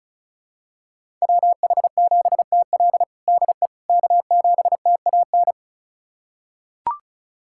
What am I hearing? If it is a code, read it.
Morse code, heard as a keyed tone, about 36 wpm, 700 Hz. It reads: WH7TL DE K7TAN
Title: USA Callsign DE Callsign - 5 Characters - 35wpm